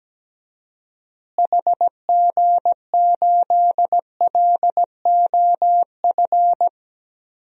Morse code, heard as a keyed tone, about 17 wpm, 700 Hz. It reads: HG8LOF
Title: DXCC-100 Call Signs - 6 Characters - Repeated 2x - 17wpm